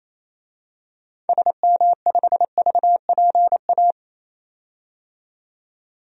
Morse code, heard as a keyed tone, about 28 wpm, 700 Hz. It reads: SM5VPA
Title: DXCC-100 Call Signs - 6 Characters - 28wpm